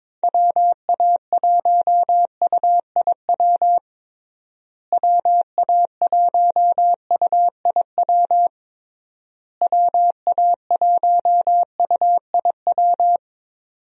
Morse code, heard as a keyed tone, about 22 wpm, 700 Hz. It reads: WA1UIW WA1UIW WA1UIW
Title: USA Call Signs - 6 Characters - Repeated 3x - 22wpm